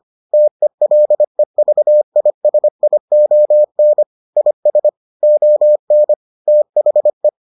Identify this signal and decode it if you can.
Morse code, heard as a keyed tone, about 25 wpm, 600 Hz. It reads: TELEVISION IS ON THE